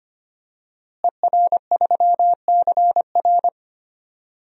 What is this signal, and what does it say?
Morse code, heard as a keyed tone, about 25 wpm, 700 Hz. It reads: ER3CR
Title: DXCC-100 Call Signs - 5 Characters - Repeated 2x - 25wpm